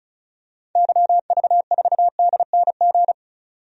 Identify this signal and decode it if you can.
Morse code, heard as a keyed tone, about 36 wpm, 700 Hz. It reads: YV4DNG
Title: DXCC-100 Call Signs - 6 Characters - 35wpm